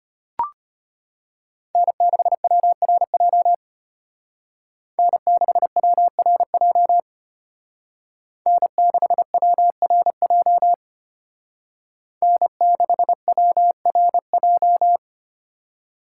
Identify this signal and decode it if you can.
Morse code, heard as a keyed tone, about 30 wpm, 700 Hz. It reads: N6WRJ N6WRJ N6WRJ N6WRJ
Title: Speed Racing - USA 1x3 Call Signs 25wpm